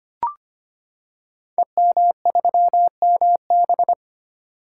Morse code, heard as a keyed tone, about 25 wpm, 700 Hz. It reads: EM3MB
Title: Code Groups - 5 Letter:Numbers - 25wpm